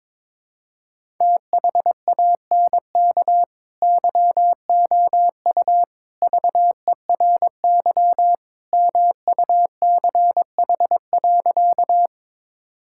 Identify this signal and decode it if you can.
Morse code, heard as a keyed tone, about 22 wpm, 700 Hz. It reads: THANK YOU VERY MUCH.
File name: practice / 22wpm / Sentences with 3-Word Phrases - 22wpm